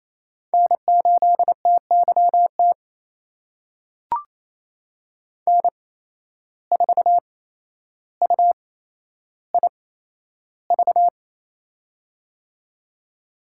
Morse code, heard as a keyed tone, about 28 wpm, spaced as for 12 wpm, 700 Hz. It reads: N8TYT N4UIV
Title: USA Call Signs - 5 Characters - 8x Character Spacing - 28wpm